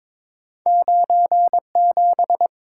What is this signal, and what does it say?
Morse code, heard as a keyed tone, about 23 wpm, 700 Hz. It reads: 97
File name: practice / 22wpm / Two Letter-Number Combinations - 22wpm